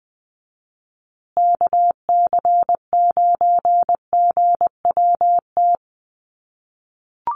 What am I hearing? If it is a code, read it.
Morse code, heard as a keyed tone, about 20 wpm, 700 Hz. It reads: KC9GWT
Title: USA Call Signs Repeated 3x 20wpm